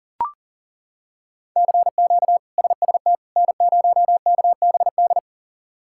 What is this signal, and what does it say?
Morse code, heard as a keyed tone, about 40 wpm, 700 Hz. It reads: CQ SST N0KBD
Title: K1USN SST QSO Practice - 40wpm